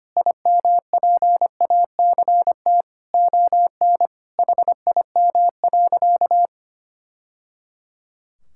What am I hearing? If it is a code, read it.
Morse code, heard as a keyed tone, about 25 wpm, 700 Hz. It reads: IMPACT ON HIM.
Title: Sentences from Top 1000 Words - Review Part 6 - 25wpm